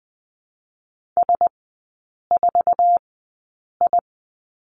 Morse code, heard as a keyed tone, about 20 wpm, spaced as for 8 wpm, 700 Hz. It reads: S4I
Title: 053 Lesson 10e - Letter r - Three Characters 20wpm